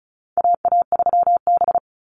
Morse code, heard as a keyed tone, about 34 wpm, 700 Hz. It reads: AA3B
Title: Call Signs Repeated 2x - 35wpm